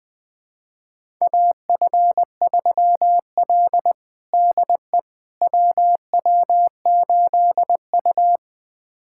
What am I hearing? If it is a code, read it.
Morse code, heard as a keyed tone, about 21 wpm, 700 Hz. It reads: AF3L DE WW8U
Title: USA Callsign DE Callsign - 4 Characters - 20wpm